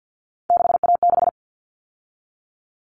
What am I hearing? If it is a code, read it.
Morse code, heard as a keyed tone, about 50 wpm, 700 Hz. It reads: 6A6
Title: All Cut Numbers - 3-Digit 50wpm